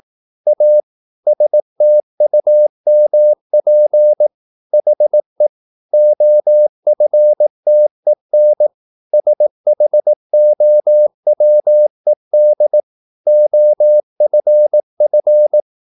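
Morse code, heard as a keyed tone, about 18 wpm, 600 Hz. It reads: A STUMP HE OFTEN SHOWED OFF
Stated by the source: Morse practice at 18 wpm, English text, from CW med Gnister.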